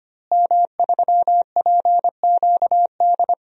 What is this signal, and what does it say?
Morse code, heard as a keyed tone, about 25 wpm, 700 Hz. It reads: M3PQD
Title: DXCC-100 Call Signs - 5 Characters - Repeated 3x - 25wpm